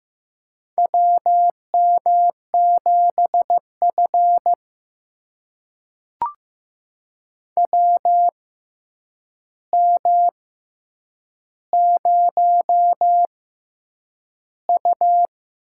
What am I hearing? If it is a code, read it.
Morse code, heard as a keyed tone, about 15 wpm, spaced as for 8 wpm, 700 Hz. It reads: WM7F WM0U
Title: USA Call Signs - 4 Characters - 6x Character Spacing - 15wpm